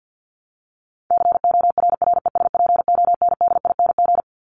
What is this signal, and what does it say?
Morse code, heard as a keyed tone, about 50 wpm, 700 Hz. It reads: CORRESPONDING
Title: ING Suffix - Top 100 Words with Base Words 50wpm